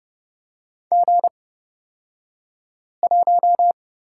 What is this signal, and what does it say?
Morse code, heard as a keyed tone, about 30 wpm, spaced as for 5 wpm, 700 Hz. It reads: G1
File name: Lesson 37d - Letter x - Two Characters 30wpm